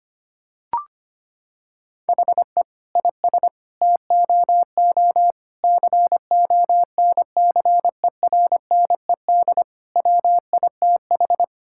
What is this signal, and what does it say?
Morse code, heard as a keyed tone, about 25 wpm, 700 Hz. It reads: HE IS TOO CONCERNED WITH
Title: Sentences from Top 1000 Words - Review Part 6 - 25wpm